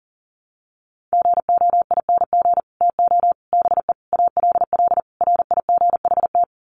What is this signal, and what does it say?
Morse code, heard as a keyed tone, about 40 wpm, 700 Hz. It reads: GOING TO BE ALL RIGHT
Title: Sets of 5 Words Encore2 40wpm